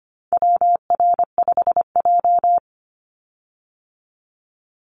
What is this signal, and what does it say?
Morse code, heard as a keyed tone, about 25 wpm, 700 Hz. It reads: WR5J
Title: Call Signs - 25wpm